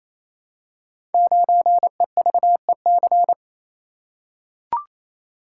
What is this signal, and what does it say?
Morse code, heard as a keyed tone, about 27 wpm, 700 Hz. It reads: 9EVEC
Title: Code Groups - 5 Letter:Numbers - 28wpm